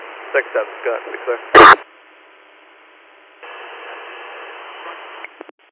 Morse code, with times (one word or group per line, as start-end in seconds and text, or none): none